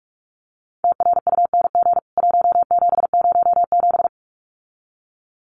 Morse code, heard as a keyed tone, about 44 wpm, 700 Hz. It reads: TRUNG 1707
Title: CWT Exchange 45wpm